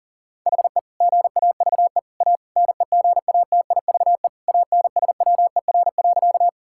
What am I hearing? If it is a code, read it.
Morse code, heard as a keyed tone, about 40 wpm, 700 Hz. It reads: HE GAVE A NEGATIVE ANSWER.
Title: Sentences from Top 1000 Words - Review Part 6 - 40wpm